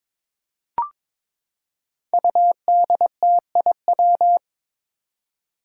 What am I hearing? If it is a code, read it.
Morse code, heard as a keyed tone, about 22 wpm, 700 Hz. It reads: UDTIW
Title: Five Challenging Letter-Number Combinations 22wpm